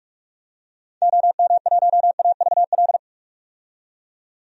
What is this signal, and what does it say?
Morse code, heard as a keyed tone, about 46 wpm, 700 Hz. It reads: OM1AVL
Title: DXCC-100 Call Signs - 6 Characters - 45wpm